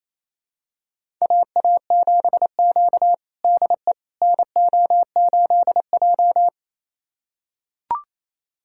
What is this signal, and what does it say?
Morse code, heard as a keyed tone, about 28 wpm, 700 Hz. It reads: AA7Q DE NO8J
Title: USA Callsign DE Callsign - 4 Characters - 28wpm